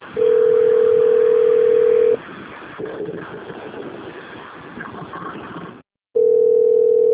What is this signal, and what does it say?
Ringback tone (North America), 440 + 480 Hz.
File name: rg-1600-unknown- - - .41627